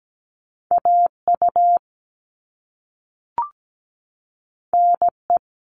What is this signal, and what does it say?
Morse code, heard as a keyed tone, about 17 wpm, 700 Hz. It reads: AU NE
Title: All Cut Numbers - 2-Digit 17wpm